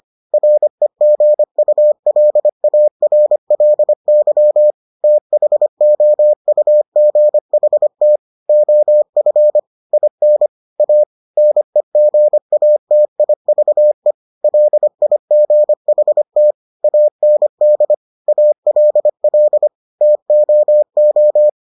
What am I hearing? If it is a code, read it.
Morse code, heard as a keyed tone, about 25 wpm, 600 Hz. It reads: REGULARLY THOUGHT OF IN A NEGATIVE LIGHT AND ALL TOO